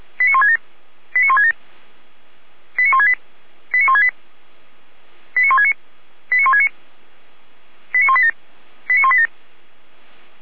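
Five-tone selective call (ZVEI), heard as quick runs of five tones, tones of about 80 ms. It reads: 89167 89167 89168 89168 89169 89169 89187 89187
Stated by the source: ZVEI five-tone selective call received by SDR.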